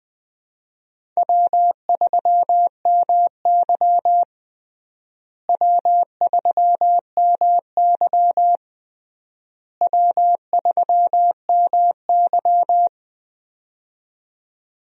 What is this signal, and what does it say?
Morse code, heard as a keyed tone, about 20 wpm, 700 Hz. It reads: W3MY W3MY W3MY
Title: USA Call Signs - 4 Characters - Repeated 3x - 20wpm